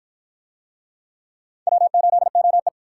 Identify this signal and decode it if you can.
Morse code, heard as a keyed tone, about 53 wpm, 700 Hz. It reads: W8OE